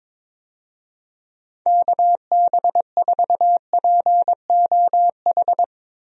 Morse code, heard as a keyed tone, about 22 wpm, 700 Hz. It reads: KB4POH
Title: USA Call Signs - 6 Characters - Repeated 3x - 22wpm